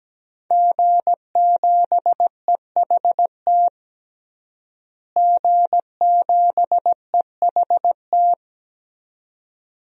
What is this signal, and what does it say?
Morse code, heard as a keyed tone, about 17 wpm, 700 Hz. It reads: G7EHT G7EHT
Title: DXCC-100 Call Signs - 5 Characters - Repeated 2x - 17wpm